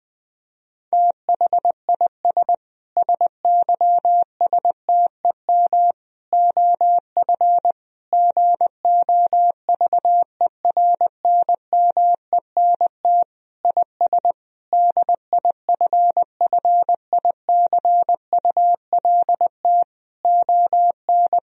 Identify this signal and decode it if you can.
Morse code, heard as a keyed tone, about 20 wpm, 700 Hz. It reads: THIS SYSTEM OF GOVERNMENT IS DIFFICULT ON
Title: Sentences from Top 1000 Words - Review Part 3 - 20wpm